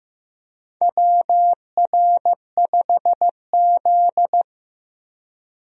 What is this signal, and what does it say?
Morse code, heard as a keyed tone, about 16 wpm, 700 Hz. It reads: WR5Z
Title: USA Call Signs - 4 Characters - 15wpm